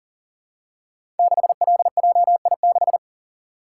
Morse code, heard as a keyed tone, about 44 wpm, 690 Hz. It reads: 6LJI6
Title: Code Groups - 5 Letter:Numbers - 40wpm